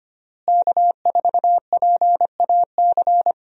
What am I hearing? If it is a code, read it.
Morse code, heard as a keyed tone, about 25 wpm, 700 Hz. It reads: K4PAC
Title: USA Call Signs - 5 Characters - Repeated 3x - 25wpm